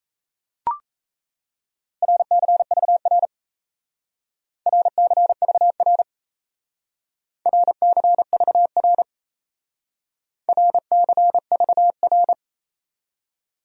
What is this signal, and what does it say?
Morse code, heard as a keyed tone, about 35 wpm, 700 Hz. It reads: RCVR RCVR RCVR RCVR
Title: Speed-Racing Abbreviations - 28wpm